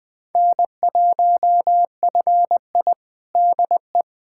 Morse code, heard as a keyed tone, about 20 wpm, 700 Hz. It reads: N1FI DE
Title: USA Callsign DE Callsign - 4 Characters - 20wpm